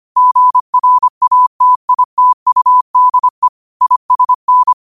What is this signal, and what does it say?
Morse code, heard as a keyed tone, about 25 wpm, 1000 Hz. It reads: GRATITUDE ISN
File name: qotdfiles / 25wpm